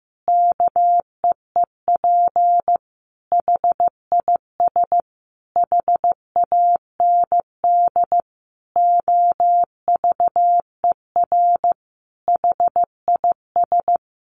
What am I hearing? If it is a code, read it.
Morse code, heard as a keyed tone, about 15 wpm, 700 Hz. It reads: KEEP HIS HAND OVER HIS
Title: Sentences from Top 1200 Words 15wpm